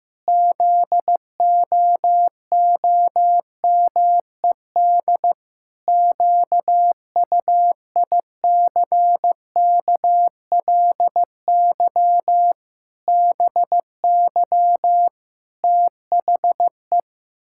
Morse code, heard as a keyed tone, about 16 wpm, 700 Hz. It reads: ZOOMED QUICKLY BY THE
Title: Pangram Challenge v2 - 15wpm